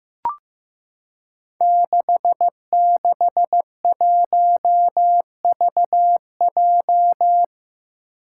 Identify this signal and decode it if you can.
Morse code, heard as a keyed tone, about 15 wpm, 700 Hz. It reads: 661VJ
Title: Five Challenging Letter-Number Combinations 15wpm